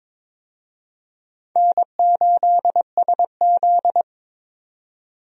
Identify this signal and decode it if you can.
Morse code, heard as a keyed tone, about 22 wpm, 700 Hz. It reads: N8SZ
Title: USA Call Signs - 4 Characters - 22wpm